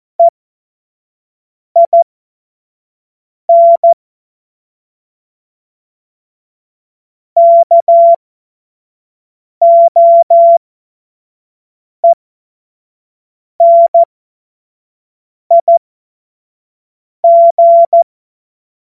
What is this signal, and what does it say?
Morse code, heard as a keyed tone, about 14 wpm, spaced as for 5 wpm, 680 Hz. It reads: EIN KOENIG
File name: Morsekurs / Witze / Witz 04. Oktober 2022 25 F70